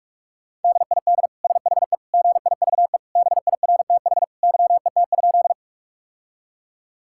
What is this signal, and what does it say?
Morse code, heard as a keyed tone, about 46 wpm, 700 Hz. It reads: DID SHE GIVE BIRTH YET?
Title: Sentences from Top 1000 Words - Review Part 5 - 45wpm